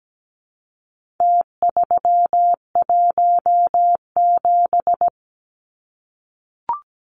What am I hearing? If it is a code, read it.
Morse code, heard as a keyed tone, about 17 wpm, 700 Hz. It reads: T317
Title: Numbers - Contest Style - 4-Digit 17wpm